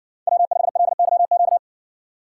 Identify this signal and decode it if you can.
Morse code, heard as a keyed tone, about 60 wpm, 700 Hz. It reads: W5LYQ